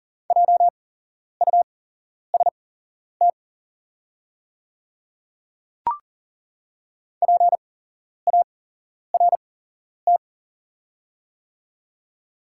Morse code, heard as a keyed tone, about 40 wpm, spaced as for 11 wpm, 700 Hz. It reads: JUST PART
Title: 8x Character Spacing - Top 300 Words - 40wpm